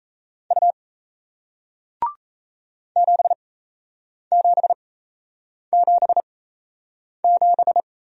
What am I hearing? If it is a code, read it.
Morse code, heard as a keyed tone, about 38 wpm, spaced as for 8 wpm, 700 Hz. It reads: U 7777